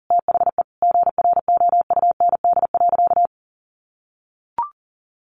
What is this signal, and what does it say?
Morse code, heard as a keyed tone, about 40 wpm, 700 Hz. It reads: THE GROUND.
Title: Sentences from Top 1500 Words 40wpm